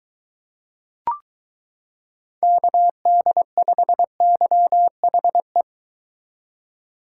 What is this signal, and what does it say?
Morse code, heard as a keyed tone, about 22 wpm, 700 Hz. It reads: KD5YHE